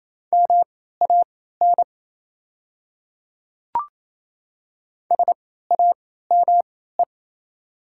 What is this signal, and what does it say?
Morse code, heard as a keyed tone, about 27 wpm, spaced as for 16 wpm, 690 Hz. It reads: MAN SAME
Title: 3x Character Spacing - Top 200 Words - 28wpm